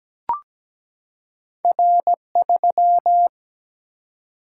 Morse code, heard as a keyed tone, about 17 wpm, 700 Hz. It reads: R3
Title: Single-Letter Single-Number 17wpm